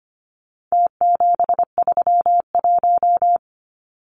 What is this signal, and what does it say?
Morse code, heard as a keyed tone, about 25 wpm, 700 Hz. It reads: T731
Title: Numbers - Contest Style - 4-Digit 25wpm